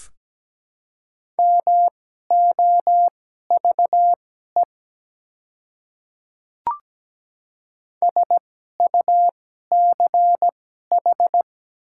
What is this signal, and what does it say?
Morse code, heard as a keyed tone, about 17 wpm, spaced as for 12 wpm, 700 Hz. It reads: MOVE SUCH
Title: 2x Character Spacing - Top 200 Words - 17wpm